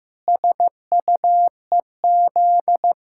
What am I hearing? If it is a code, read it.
Morse code, heard as a keyed tone, about 15 wpm, 700 Hz. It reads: SUEZ
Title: Pangram Challenge v2 - 15wpm